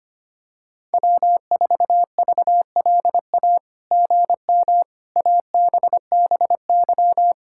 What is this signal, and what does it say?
Morse code, heard as a keyed tone, about 25 wpm, 700 Hz. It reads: W4VLA GM ABBY
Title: SOTA QSO Practice 25wpm